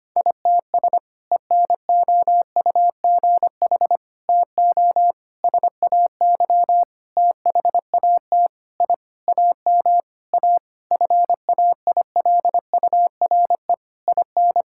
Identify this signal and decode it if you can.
Morse code, heard as a keyed tone, about 25 wpm, 700 Hz. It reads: ITS ENOUGH TO SAY THAT I AM A FAILURE IN
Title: Sentences from Top 1000 Words - Review Part 6 - 25wpm